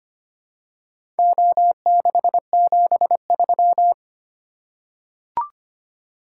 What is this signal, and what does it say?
Morse code, heard as a keyed tone, about 25 wpm, 700 Hz. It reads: O673
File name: practice / 25wpm / Four Letter-Number Combinations - 25wpm